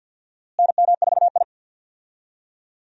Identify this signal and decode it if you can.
Morse code, heard as a keyed tone, about 46 wpm, 700 Hz. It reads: NM4I
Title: USA Call Signs - 4 Characters - Repeated 3x - 50wpm